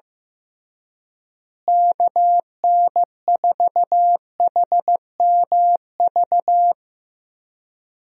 Morse code, heard as a keyed tone, about 15 wpm, 700 Hz. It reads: KN4HMV
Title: USA Call Signs - 6 Characters - Repeated 3x - 15wpm